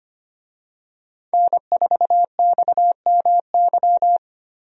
Morse code, heard as a keyed tone, about 25 wpm, 700 Hz. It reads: N4XMY
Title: USA Call Signs - 5 Characters - Repeated 3x - 25wpm